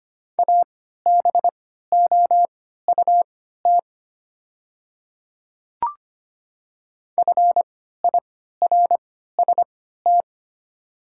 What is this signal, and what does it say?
Morse code, heard as a keyed tone, about 25 wpm, spaced as for 14 wpm, 700 Hz. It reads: ABOUT FIRST
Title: 3x Character Spacing - Top 100 Words - 25wpm